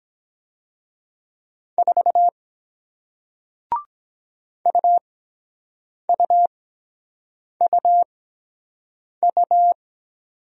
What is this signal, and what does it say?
Morse code, heard as a keyed tone, about 23 wpm, spaced as for 7 wpm, 700 Hz. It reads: 4 UUUU